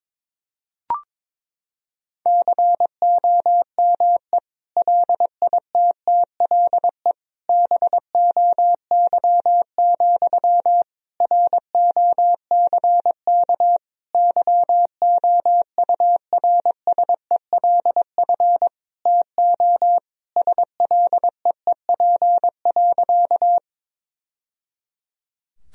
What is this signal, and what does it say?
Morse code, heard as a keyed tone, about 22 wpm, 700 Hz. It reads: COME LITTLE BOY, ROCK YOURSELF TO SLEEP.
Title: Sentences from Top 1000 Words - Review Part 2 - 22wpm